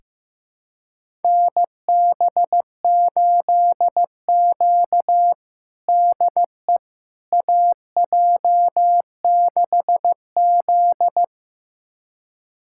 Morse code, heard as a keyed tone, about 16 wpm, 700 Hz. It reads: NB8Q DE AJ6Z
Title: USA Callsign DE Callsign - 4 Characters - 15wpm